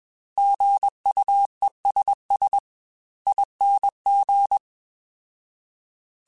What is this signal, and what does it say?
Morse code, heard as a keyed tone, about 21 wpm, spaced as for 18 wpm, 800 Hz. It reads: GUESSING
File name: Alphabetical / Alphabetical mp3s / Gs @20